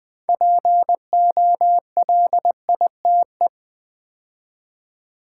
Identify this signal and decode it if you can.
Morse code, heard as a keyed tone, about 20 wpm, 700 Hz. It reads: POLITE
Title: 6 Letter Words - First 3 Letters Given - 20wpm